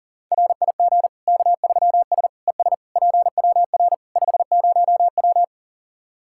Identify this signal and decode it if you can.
Morse code, heard as a keyed tone, about 40 wpm, 700 Hz. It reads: RIG K3S ES PWR 50W
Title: K1USN SST QSO Practice 40wpm